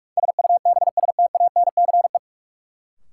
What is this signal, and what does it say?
Morse code, heard as a keyed tone, about 46 wpm, 700 Hz. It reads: SUBSTANCE
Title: Sentences from Top 1500 Words - New Word Drill Alt - 45wpm